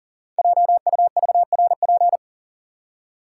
Morse code, heard as a keyed tone, about 39 wpm, 700 Hz. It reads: JUVRP
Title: Code Groups - 5 Letters - 40wpm